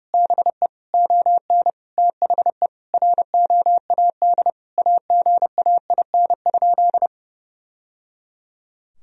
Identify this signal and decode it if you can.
Morse code, heard as a keyed tone, about 32 wpm, 700 Hz. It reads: BE ON THE ROAD AGAIN?
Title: Sentences from Top 1000 Words - Review Part 2 - 30wpm